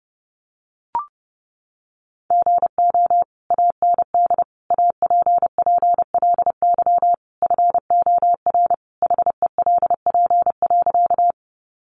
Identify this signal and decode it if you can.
Morse code, heard as a keyed tone, about 30 wpm, 700 Hz. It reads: GO AND APPLY FOR HELP.
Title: Sentences from Top 1200 Words 30wpm